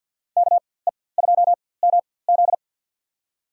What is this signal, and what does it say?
Morse code, heard as a keyed tone, about 52 wpm, spaced as for 23 wpm, 700 Hz. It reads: KE2MZ
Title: USA Call Signs - 5 Characters - 4x Character Spacing - 50wpm